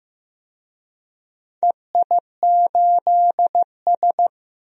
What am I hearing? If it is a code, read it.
Morse code, heard as a keyed tone, about 15 wpm, 700 Hz. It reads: EI8S
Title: DXCC-100 Call Signs - 4 Characters - 15wpm